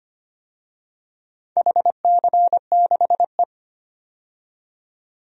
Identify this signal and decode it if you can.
Morse code, heard as a keyed tone, about 25 wpm, 700 Hz. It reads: HC6E
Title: DXCC-100 Call Signs - 4 Characters - 25wpm